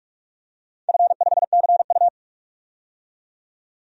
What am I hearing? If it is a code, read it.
Morse code, heard as a keyed tone, about 46 wpm, 700 Hz. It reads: F5CU
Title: DXCC-100 Call Signs - 4 Characters - 45wpm